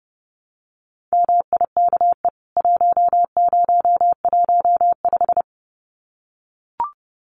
Morse code, heard as a keyed tone, about 30 wpm, 700 Hz. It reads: MIKE 1015
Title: CWT Exchange 30wpm